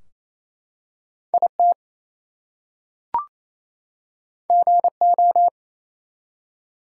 Morse code, heard as a keyed tone, about 27 wpm, 700 Hz. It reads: IT GO
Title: Two Letter Words - 28wpm